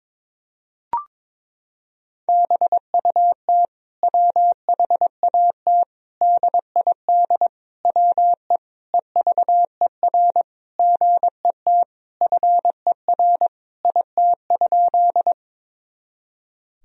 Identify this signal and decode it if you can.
Morse code, heard as a keyed tone, about 21 wpm, 700 Hz. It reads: BUT WHAT DID WE EVER GET FER IT?
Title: Sentences from Top 400 Words and Abbreviations 22wpm